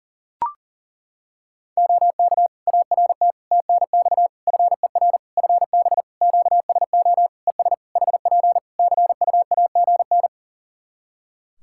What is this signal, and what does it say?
Morse code, heard as a keyed tone, about 40 wpm, 700 Hz. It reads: OK ART TNX FER FB QSO ES HP CUAGN
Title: Practice QSO - Exchange 4 - Ending - 40wpm